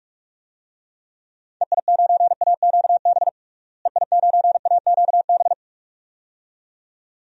Morse code, heard as a keyed tone, about 44 wpm, 700 Hz. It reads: EI9AQB EI9AQB
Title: DXCC-100 Call Signs - 6 Characters - Repeated 2x - 45wpm